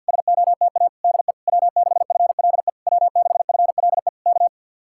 Morse code, heard as a keyed tone, about 66 wpm, spaced as for 58 wpm, 700 Hz. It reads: SOTA DE W6FLE W6FLE K
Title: Speed-Racing Common QSO Phrases Encore - 50wpm